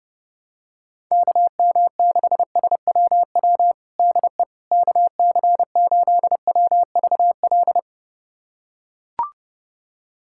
Morse code, heard as a keyed tone, about 30 wpm, 700 Hz. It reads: KM6SWW DE KC8WVL
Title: USA Callsign DE Callsign 30wpm